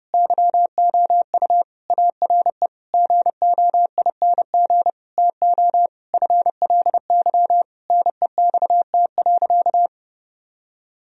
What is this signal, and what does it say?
Morse code, heard as a keyed tone, about 30 wpm, 700 Hz. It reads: YOU ARE GOING TO FLY NEXT.
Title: Sentences from Top 1000 Words - Review Part 2 - 30wpm